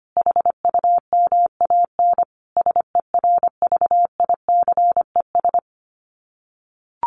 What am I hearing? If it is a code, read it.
Morse code, heard as a keyed tone, about 25 wpm, 700 Hz. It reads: HUMAN SERVICES
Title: Sets of 5 Words 25wpm